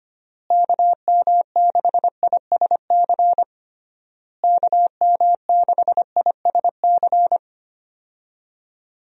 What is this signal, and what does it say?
Morse code, heard as a keyed tone, about 25 wpm, 700 Hz. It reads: KM6ISC KM6ISC
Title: USA Call Signs - 6 Characters - Repeated 3x - 25wpm